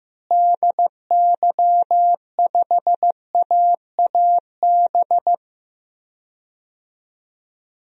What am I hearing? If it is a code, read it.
Morse code, heard as a keyed tone, about 15 wpm, 700 Hz. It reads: DY5AAB
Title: DXCC-100 Call Signs - 6 Characters - Repeated 2x - 15wpm